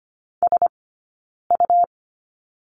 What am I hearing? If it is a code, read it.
Morse code, heard as a keyed tone, about 25 wpm, 700 Hz. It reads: S U
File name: Lesson 17d - Letter u - Two Characters 25wpm